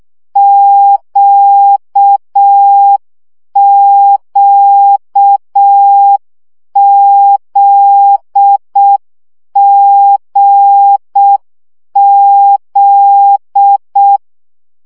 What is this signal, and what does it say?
Morse code, heard as a keyed tone, about 6 wpm, 800 Hz. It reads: QQZGZ
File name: CW gyakorlas g z q 5 baud